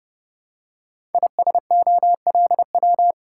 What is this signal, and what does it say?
Morse code, heard as a keyed tone, about 30 wpm, 700 Hz. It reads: ISOLW
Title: Code Groups - 5 Letter:Numbers - 30wpm